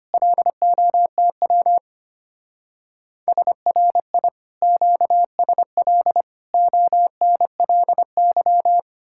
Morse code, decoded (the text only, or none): LOTW SRI QSL ONLY